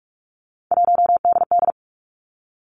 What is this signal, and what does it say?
Morse code, heard as a keyed tone, about 46 wpm, 700 Hz. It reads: 1DD
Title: Three Challenging Letter-Number Combinations 45wpm